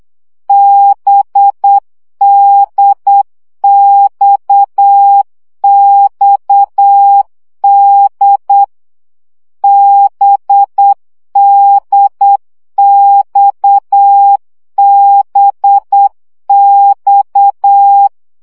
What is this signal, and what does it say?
Morse code, heard as a keyed tone, about 8 wpm, 800 Hz. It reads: BDXXD BDXBX
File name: CW gyakorlas d b x 7 baud